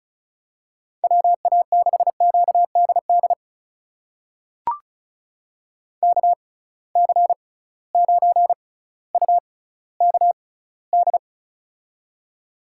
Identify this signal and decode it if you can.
Morse code, heard as a keyed tone, about 34 wpm, spaced as for 18 wpm, 700 Hz. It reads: WA6QDD KC9UKD
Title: USA Call Signs - 6 Characters - 6x Character Spacing - 35wpm